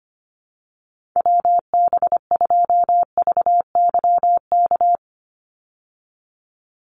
Morse code, heard as a keyed tone, about 25 wpm, 700 Hz. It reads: WB2VYK
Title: Call Signs Repeated 2x - 25wpm